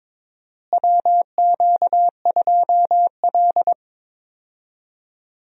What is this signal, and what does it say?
Morse code, heard as a keyed tone, about 22 wpm, 700 Hz. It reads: WQ2L
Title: USA Call Signs - 4 Characters - Repeated 2x - 22wpm